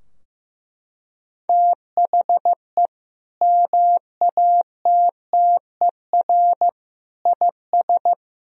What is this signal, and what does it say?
Morse code, heard as a keyed tone, about 16 wpm, 700 Hz. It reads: THE MATTER IS
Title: Sentences from Top 1000 Words - Review Part 4 - 15wpm